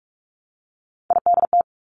Morse code, heard as a keyed tone, about 46 wpm, 700 Hz. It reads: IDT
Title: Three Challenging Letter-Number Combinations 45wpm